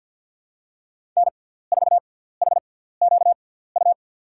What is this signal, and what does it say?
Morse code, heard as a keyed tone, about 53 wpm, spaced as for 17 wpm, 690 Hz. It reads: N4HQU
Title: USA Call Signs - 6x Character Spacing - 50wpm